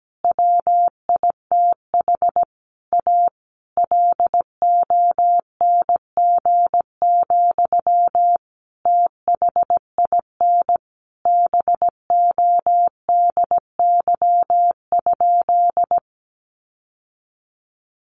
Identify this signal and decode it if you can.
Morse code, heard as a keyed tone, about 17 wpm, 700 Hz. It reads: WITH A LONG, THIN BODY?
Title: Sentences from Top 1200 Words 17wpm